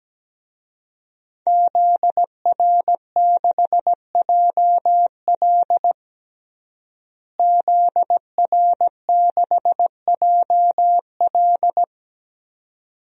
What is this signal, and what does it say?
Morse code, heard as a keyed tone, about 17 wpm, 700 Hz. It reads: ZR6JL ZR6JL
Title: DXCC-100 Call Signs - 5 Characters - Repeated 2x - 17wpm